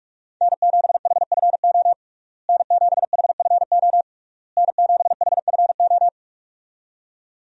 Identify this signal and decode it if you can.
Morse code, heard as a keyed tone, about 41 wpm, 700 Hz. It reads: N7HFO N7HFO N7HFO
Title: USA Call Signs - 5 Characters - Repeated 3x - 45wpm